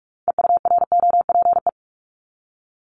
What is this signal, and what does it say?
Morse code, heard as a keyed tone, about 46 wpm, 700 Hz. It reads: EUROPE
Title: Multi-letter Phonemes Repeated 3x 45wpm